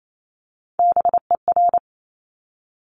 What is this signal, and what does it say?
Morse code, heard as a keyed tone, about 28 wpm, 700 Hz. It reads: BER
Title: Top Three Letter Combinations in Words 28wpm